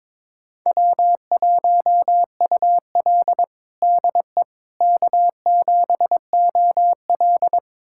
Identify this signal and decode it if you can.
Morse code, heard as a keyed tone, about 22 wpm, 700 Hz. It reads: W1UL DE K7OL
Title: USA Callsign DE Callsign - 4 Characters - 22wpm